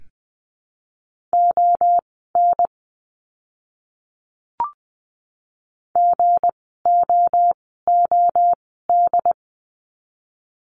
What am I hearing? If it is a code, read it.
Morse code, heard as a keyed tone, about 20 wpm, spaced as for 14 wpm, 700 Hz. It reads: ON GOOD
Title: 160 Lesson 26f - Letter g - Words 20wpm